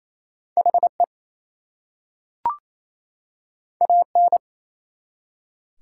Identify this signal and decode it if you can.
Morse code, heard as a keyed tone, about 27 wpm, 700 Hz. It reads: HE AN
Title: Two Letter Words - 28wpm